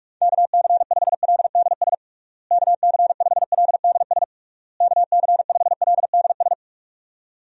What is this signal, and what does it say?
Morse code, heard as a keyed tone, about 44 wpm, 700 Hz. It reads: KC5LDS KC5LDS KC5LDS
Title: USA Call Signs - 6 Characters - Repeated 3x - 45wpm